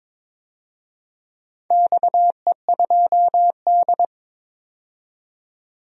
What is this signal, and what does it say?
Morse code, heard as a keyed tone, about 22 wpm, 700 Hz. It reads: XE2D
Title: DXCC-100 Call Signs - 4 Characters - 22wpm